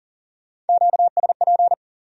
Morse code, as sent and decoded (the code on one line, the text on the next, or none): --.- ... .--.
QSP